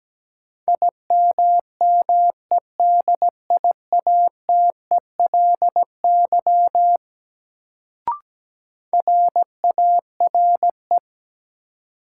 Morse code, heard as a keyed tone, about 17 wpm, 700 Hz. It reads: IMMEDIATELY RARE
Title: Sentences from Top 1100 Words - New Word Drill - 17wpm